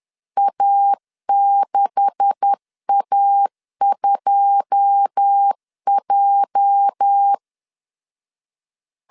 Morse code, heard as a keyed tone, about 10 wpm, 800 Hz. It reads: A6A2J
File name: CW / DET / 1 9 1500C S B 10wpm 4439